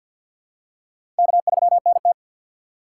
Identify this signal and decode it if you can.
Morse code, heard as a keyed tone, about 52 wpm, 700 Hz. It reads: K3NT
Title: USA Call Signs - 4 Characters - 2x Character Spacing - 50wpm